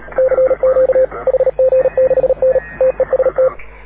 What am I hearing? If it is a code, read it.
Morse code, heard as a keyed tone, about 36 wpm, 530 Hz. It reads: CQ HG6N TEST
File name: AUDIO2019 / HG6N CW SSB MIX